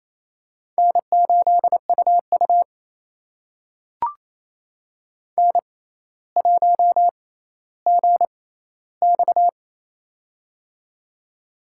Morse code, heard as a keyed tone, about 28 wpm, spaced as for 14 wpm, 700 Hz. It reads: N8UU N1GX
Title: USA Call Signs - 4 Characters - 6x Character Spacing - 28wpm